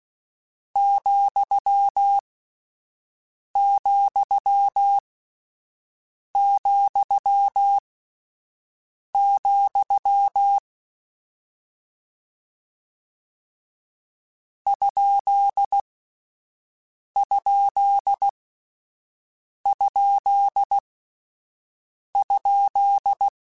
Morse code, heard as a keyed tone, about 16 wpm, spaced as for 5 wpm, 800 Hz. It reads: ,,,, ????
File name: Misc / Misc-mp3s / Punct @5